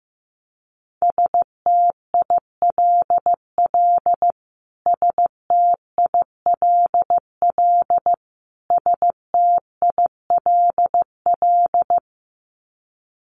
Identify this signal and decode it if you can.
Morse code, heard as a keyed tone, about 15 wpm, 700 Hz. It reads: STILL STILL STILL
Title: Top 500 Words Repeated 15wpm